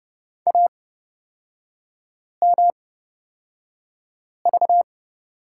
Morse code, heard as a keyed tone, about 30 wpm, spaced as for 5 wpm, 700 Hz. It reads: AMV
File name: Lesson 36e - BT - Three Characters 30wpm